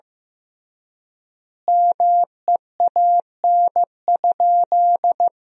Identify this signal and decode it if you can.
Morse code, heard as a keyed tone, about 15 wpm, 700 Hz. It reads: MEAN?
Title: 3x Word-Spacing - Sentences from Top 500 Words - 15wpm